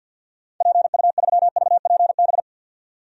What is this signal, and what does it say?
Morse code, heard as a keyed tone, about 52 wpm, 700 Hz. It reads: PU3VPB
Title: DXCC-100 Call Signs - 6 Characters - 50wpm